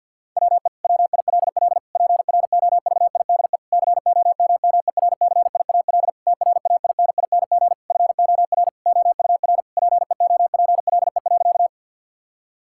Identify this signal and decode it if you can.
Morse code, heard as a keyed tone, about 50 wpm, 700 Hz. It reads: WE WILL PROVIDE COMMERCIAL TRAINING FOR OUR PEOPLE.
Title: Sentences from Top 1000 Words - Review Part 5 - 50wpm